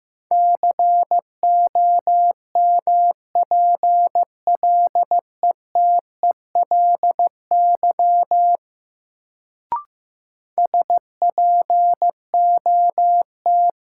Morse code, heard as a keyed tone, about 15 wpm, 700 Hz. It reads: COMPLETELY SPOT
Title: Sentences from Top 1100 Words - New Word Drill Alt - 15wpm